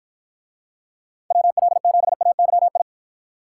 Morse code, heard as a keyed tone, about 47 wpm, 700 Hz. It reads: WL7AYI